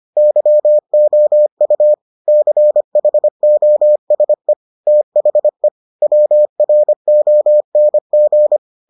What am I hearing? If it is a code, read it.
Morse code, heard as a keyed tone, about 25 wpm, 600 Hz. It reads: YOU CHOSE THE WRONG